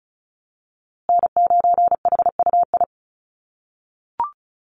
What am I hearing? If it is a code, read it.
Morse code, heard as a keyed tone, about 34 wpm, 700 Hz. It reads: N9HUI
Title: USA Call Signs Repeated 3x 35wpm